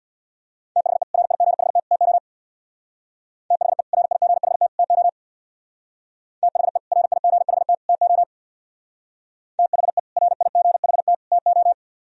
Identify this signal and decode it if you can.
Morse code, heard as a keyed tone, about 61 wpm, 700 Hz. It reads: THE RIGHT TO THE RIGHT TO THE RIGHT TO THE RIGHT TO
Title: Speed-Racing - Sets of 3 Words 50wpm